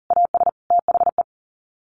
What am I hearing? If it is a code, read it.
Morse code, heard as a keyed tone, about 40 wpm, 700 Hz. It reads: AS THE
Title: Sets of 3 Words 40wpm